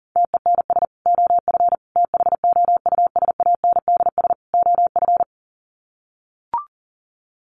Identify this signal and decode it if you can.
Morse code, heard as a keyed tone, about 40 wpm, 700 Hz. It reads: TENS OF THOUSANDS OF
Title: Sets of 4 Words 40wpm